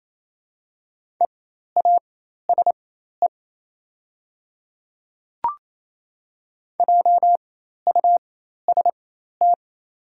Morse code, heard as a keyed tone, about 28 wpm, spaced as for 13 wpm, 700 Hz. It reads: EASE JUST
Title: 4x Character Spacing - Top 300 Words - 28wpm